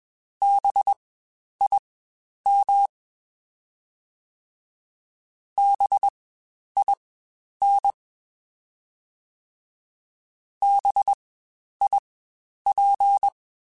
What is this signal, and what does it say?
Morse code, heard as a keyed tone, about 21 wpm, spaced as for 10 wpm, 800 Hz. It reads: BIM BIN BIP
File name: Spaced Alphas / Spaced Alphas mp3s / Bs sp @20